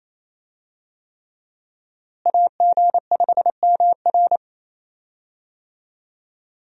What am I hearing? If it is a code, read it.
Morse code, heard as a keyed tone, about 27 wpm, 700 Hz. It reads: AG5MR
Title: USA Call Signs - 5 Characters - Repeated 3x - 28wpm